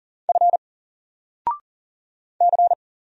Morse code, heard as a keyed tone, about 40 wpm, 700 Hz. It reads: F C
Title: Single Character (Letter:Number:Punctuation:Prosign) Alt 40wpm